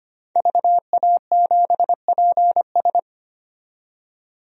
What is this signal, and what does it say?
Morse code, heard as a keyed tone, about 25 wpm, 700 Hz. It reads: VA7PS
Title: DXCC-100 Call Signs - 5 Characters - 25wpm